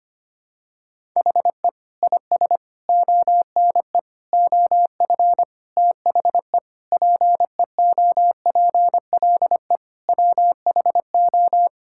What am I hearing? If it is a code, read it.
Morse code, heard as a keyed tone, about 25 wpm, 700 Hz. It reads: HE IS ONE OF THE PEOPLE WHO
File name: practice / 25wpm / Sentences with 4-Word Phrases - 25wpm